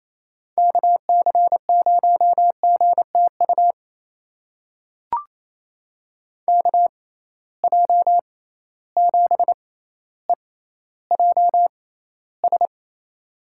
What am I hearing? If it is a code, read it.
Morse code, heard as a keyed tone, about 28 wpm, spaced as for 14 wpm, 700 Hz. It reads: KC0GTU KJ7EJS
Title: USA Call Signs - 6 Characters - 6x Character Spacing - 28wpm